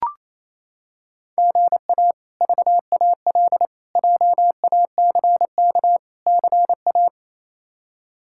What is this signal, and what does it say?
Morse code, heard as a keyed tone, about 28 wpm, 700 Hz. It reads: GA VAL JACK CA
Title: K1USN SST QSO Practice - 28wpm